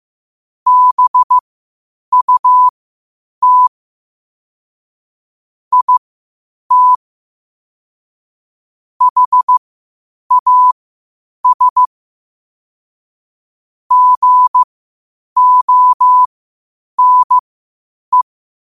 Morse code, heard as a keyed tone, about 15 wpm, spaced as for 8 wpm, 1000 Hz. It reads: BUT IT HAS GONE
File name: qotdfiles / 8wpm